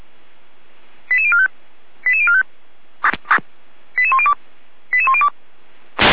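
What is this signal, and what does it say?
Five-tone selective call (ZVEI), heard as quick runs of five tones, tones of about 70 ms. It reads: 88045 88045 88102 88102
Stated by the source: ZVEI five-tone selective call received by SDR.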